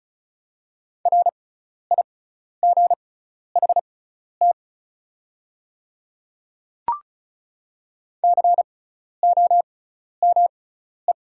Morse code, heard as a keyed tone, about 35 wpm, spaced as for 12 wpm, 690 Hz. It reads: RIGHT COME
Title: 6x Character Spacing - Top 200 Words - 35wpm